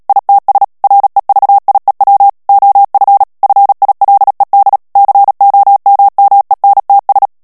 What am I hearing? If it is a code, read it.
Morse code, heard as a keyed tone, about 37 wpm, 800 Hz. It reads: ITS REVIEW OF FILED COMMENTS
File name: morse mp3 / ANGLAIS texte clair 3 vitesse 35 mots minute